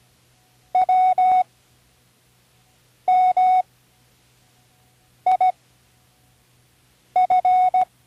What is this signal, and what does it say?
Morse code, heard as a keyed tone, about 16 wpm, spaced as for 5 wpm, 740 Hz. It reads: WMIF
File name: CWResources / Learning Morse Code (from Tune in the World) / 19 random